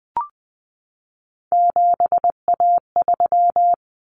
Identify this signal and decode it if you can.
Morse code, heard as a keyed tone, about 20 wpm, 700 Hz. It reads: 7A3
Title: Common Cut Numbers - 3-Digit 20wpm